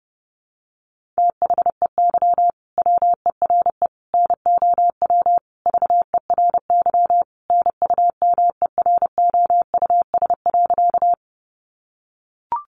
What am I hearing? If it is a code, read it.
Morse code, heard as a keyed tone, about 30 wpm, 700 Hz. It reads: THEY WERE NOW VERY NUMEROUS.
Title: Sentences from Top 1400 Words 30wpm